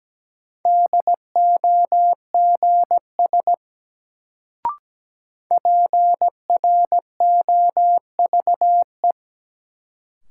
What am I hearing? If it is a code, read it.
Morse code, heard as a keyed tone, about 17 wpm, 700 Hz. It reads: DOGS PROVE
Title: Sentences from Top 1600 Words - New Word Drill Alt - 17wpm